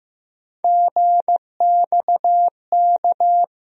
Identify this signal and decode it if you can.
Morse code, heard as a keyed tone, about 15 wpm, 700 Hz. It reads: GXK
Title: USA Call Sign Postfixes - 3 Letters 15wpm